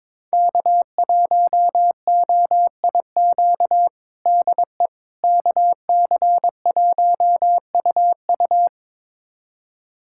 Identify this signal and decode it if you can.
Morse code, heard as a keyed tone, about 22 wpm, 700 Hz. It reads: K1OIQ DE KC1UU
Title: USA Callsign DE Callsign - 5 Characters - 22wpm